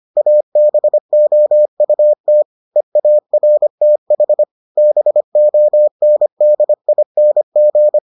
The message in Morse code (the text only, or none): ABOUT EARTH BONDING